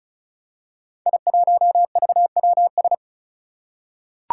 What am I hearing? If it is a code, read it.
Morse code, heard as a keyed tone, about 36 wpm, 700 Hz. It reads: I1VWS
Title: Five Challenging Letter-Number Combinations 35wpm